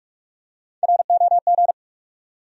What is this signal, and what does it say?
Morse code, heard as a keyed tone, about 48 wpm, 700 Hz. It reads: ROG
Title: USA Common Names - Max 3 Letters - 45wpm